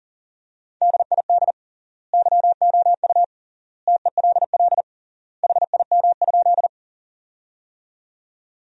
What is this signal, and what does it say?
Morse code, heard as a keyed tone, about 38 wpm, 700 Hz. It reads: DID YOU TELL HIM?
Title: 3x Word-Spacing - Sentences from Top 200 Words - 40wpm